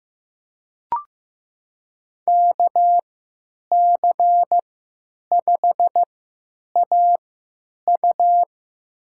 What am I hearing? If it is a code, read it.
Morse code, heard as a keyed tone, about 15 wpm, spaced as for 9 wpm, 700 Hz. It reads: KC5AU
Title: USA Call Signs - 5 Characters - 3x Character Spacing - 15wpm